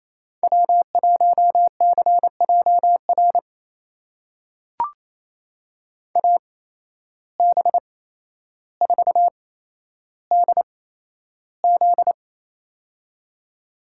Morse code, heard as a keyed tone, about 28 wpm, spaced as for 12 wpm, 700 Hz. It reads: W1CJR AB4DZ
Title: USA Call Signs - 5 Characters - 8x Character Spacing - 28wpm